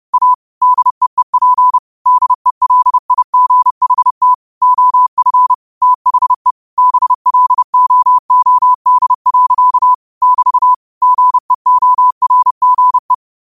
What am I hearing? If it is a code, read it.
Morse code, heard as a keyed tone, about 30 wpm, 1000 Hz. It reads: A DEEP DELIGHT OF THE BLOOD. <BT> GEORGE
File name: 30wpm